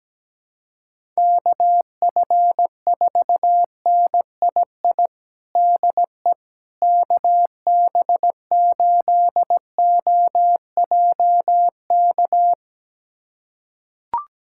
Morse code, heard as a keyed tone, about 17 wpm, 700 Hz. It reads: KF4NII DE KB8OJK
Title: USA Callsign DE Callsign - 6 Characters - 17wpm